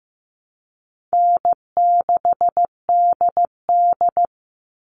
Morse code, heard as a keyed tone, about 15 wpm, 700 Hz. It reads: N6DD
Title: All Cut Numbers - 4-Digit 15wpm